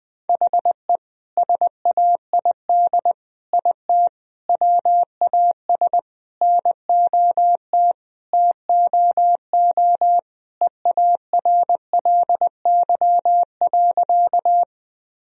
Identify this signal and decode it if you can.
Morse code, heard as a keyed tone, about 20 wpm, 700 Hz. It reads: HE SAID IT WAS NOT TOO EARLY.
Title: Sentences from Top 1000 Words - Review Part 2 - 20wpm